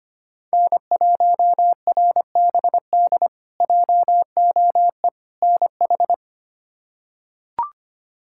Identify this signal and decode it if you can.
Morse code, heard as a keyed tone, about 25 wpm, 700 Hz. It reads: N1RBD JOE NH
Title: K1USN SST QSO Practice - 25wpm